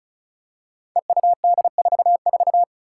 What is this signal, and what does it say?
Morse code, heard as a keyed tone, about 36 wpm, 700 Hz. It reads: EUD44
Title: Five Challenging Letter-Number Combinations 35wpm